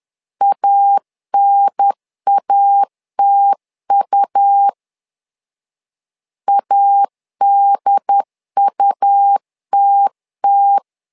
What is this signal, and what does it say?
Morse code, heard as a keyed tone, about 10 wpm, 800 Hz. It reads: ANATU ADUTT
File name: LEZIONI / 2 UHDN / 2 UHDN 10wpm 1904